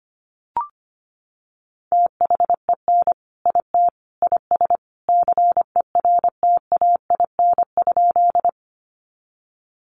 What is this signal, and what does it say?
Morse code, heard as a keyed tone, about 25 wpm, 700 Hz. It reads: THEN IT IS CERTAIN?
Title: Sentences from Top 500 Words 25wpm